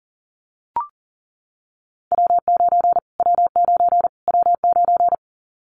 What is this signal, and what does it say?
Morse code, heard as a keyed tone, about 40 wpm, 700 Hz. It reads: W9 W9 W9
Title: Callsign Prefixes for Top 25 DXCC - Repeated 3x - 40wpm